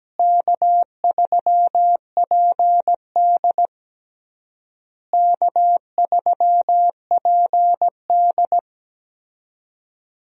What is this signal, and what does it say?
Morse code, heard as a keyed tone, about 17 wpm, 700 Hz. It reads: K3PD K3PD
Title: USA Call Signs - 4 Characters - Repeated 3x - 17wpm